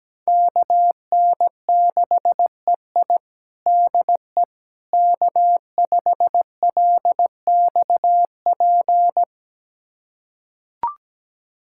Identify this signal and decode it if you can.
Morse code, heard as a keyed tone, about 17 wpm, 700 Hz. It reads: KN6EI DE K5LXP
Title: USA Callsign DE Callsign - 5 Characters - 17wpm